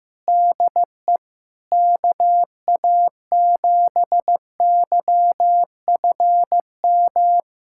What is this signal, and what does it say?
Morse code, heard as a keyed tone, about 15 wpm, 700 Hz. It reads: DE KA7YFM
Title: USA Callsign DE Callsign - 6 Characters - 15wpm